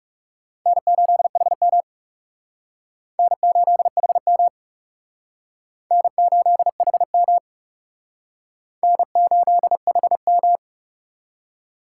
Morse code, heard as a keyed tone, about 36 wpm, 700 Hz. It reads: N8HM N8HM N8HM N8HM
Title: Speed Racing - USA 1x2 Call Signs 30wpm